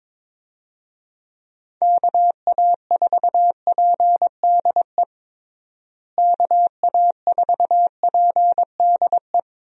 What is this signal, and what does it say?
Morse code, heard as a keyed tone, about 21 wpm, 700 Hz. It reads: KA4PDE KA4PDE
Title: USA Call Signs - 6 Characters - Repeated 2x - 22wpm